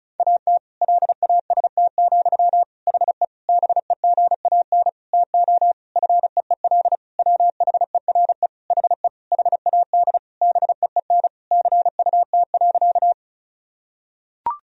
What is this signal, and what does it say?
Morse code, heard as a keyed tone, about 34 wpm, 700 Hz. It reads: AT LAST, HE BEGAN TO FEEL WHERE HE HAD BEEN CUT.
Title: Sentences from Top 1000 Words - Review Part 1 - 35wpm